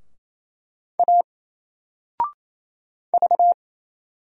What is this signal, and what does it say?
Morse code, heard as a keyed tone, about 27 wpm, 700 Hz. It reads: A V
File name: Single-Letter-Number-28wpm